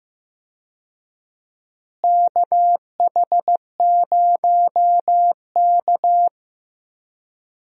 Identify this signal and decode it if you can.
Morse code, heard as a keyed tone, about 15 wpm, 700 Hz. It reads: KH0K
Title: USA Call Signs - 4 Characters - Repeated 2x - 15wpm